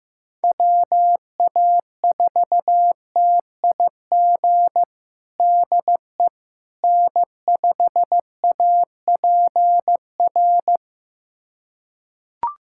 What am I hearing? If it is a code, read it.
Morse code, heard as a keyed tone, about 15 wpm, 700 Hz. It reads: WA4TIG DE N5APR
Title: USA Callsign DE Callsign 15wpm